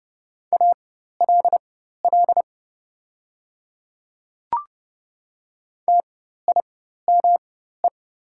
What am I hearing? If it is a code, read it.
Morse code, heard as a keyed tone, about 30 wpm, spaced as for 14 wpm, 700 Hz. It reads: ALL TIME
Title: 4x Character Spacing - Top 100 Words - 30wpm